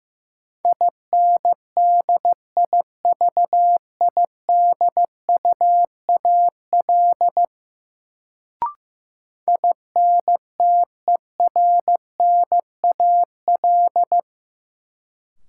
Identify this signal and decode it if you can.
Morse code, heard as a keyed tone, about 15 wpm, 700 Hz. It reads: INDIVIDUAL INTERNAL
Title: Sentences from Top 1400 Words - New Word Drill Alt - 15wpm